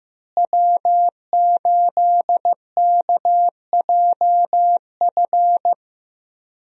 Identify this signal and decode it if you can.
Morse code, heard as a keyed tone, about 15 wpm, 700 Hz. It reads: W8KJF
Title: USA Call Signs - 5 Characters - Repeated 3x - 15wpm